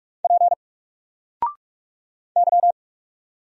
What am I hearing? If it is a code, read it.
Morse code, heard as a keyed tone, about 48 wpm, 700 Hz. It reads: P Y
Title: Single Character (Letter:Number:Punctuation:Prosign) Alt 45wpm